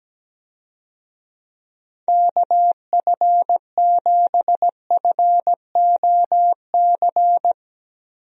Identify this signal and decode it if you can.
Morse code, heard as a keyed tone, about 17 wpm, 700 Hz. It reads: KF7FOC
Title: USA Call Signs - 6 Characters - Repeated 2x - 17wpm